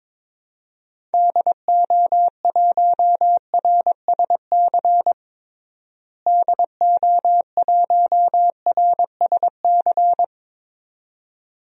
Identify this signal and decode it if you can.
Morse code, heard as a keyed tone, about 22 wpm, 700 Hz. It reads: DO1RSC DO1RSC
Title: DXCC-100 Call Signs - 6 Characters - Repeated 2x - 22wpm